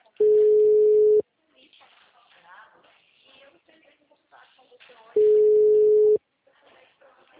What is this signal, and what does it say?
Ringback tone (Europe, 425 Hz), with speech.